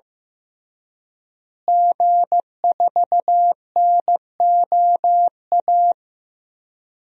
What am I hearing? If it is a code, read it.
Morse code, heard as a keyed tone, about 16 wpm, 700 Hz. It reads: G4NOA
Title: DXCC-100 Call Signs - 5 Characters - Repeated 3x - 15wpm